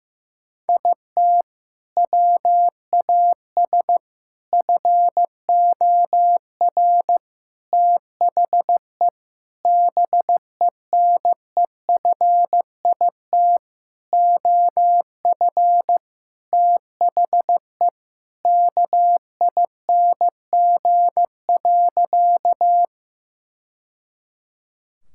Morse code, heard as a keyed tone, about 15 wpm, 700 Hz. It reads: IT WAS FOR THE BENEFIT OF THE KING.
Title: Sentences from Top 1000 Words - Review Part 6 - 15wpm